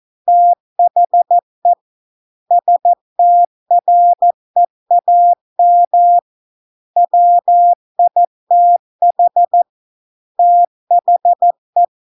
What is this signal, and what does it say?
Morse code, heard as a keyed tone, about 14 wpm, 700 Hz. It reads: THE STREAM WITH THE